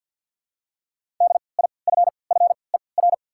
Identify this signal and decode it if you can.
Morse code, heard as a keyed tone, about 54 wpm, spaced as for 29 wpm, 700 Hz. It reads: DIFFER
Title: 3x Character Spacing - Top 300 Words - 50wpm